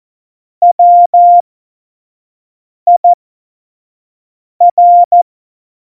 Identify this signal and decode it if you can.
Morse code, heard as a keyed tone, about 14 wpm, spaced as for 5 wpm, 700 Hz. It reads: WIR
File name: Morsekurs / Witze / Witz 29. Dezember 2020, 25 F70